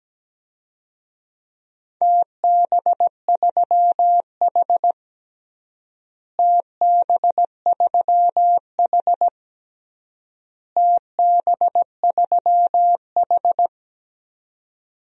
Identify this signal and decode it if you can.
Morse code, heard as a keyed tone, about 17 wpm, 700 Hz. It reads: TB3H TB3H TB3H
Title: DXCC-100 Call Signs - 4 Characters - Repeated 3x - 17wpm